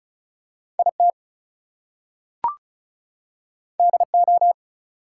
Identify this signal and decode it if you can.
Morse code, heard as a keyed tone, about 36 wpm, 700 Hz. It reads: IT DO
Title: Two Letter Words - 35wpm